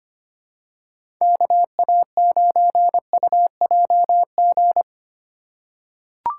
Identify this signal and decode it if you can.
Morse code, heard as a keyed tone, about 25 wpm, 700 Hz. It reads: KA9UJG
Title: USA Call Signs - 6 Characters - 4x Character Spacing - 25wpm